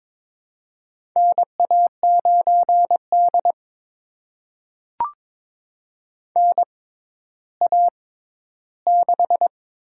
Morse code, heard as a keyed tone, about 22 wpm, spaced as for 12 wpm, 700 Hz. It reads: NA9D NA6
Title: USA Call Signs - 4 Characters - 6x Character Spacing - 22wpm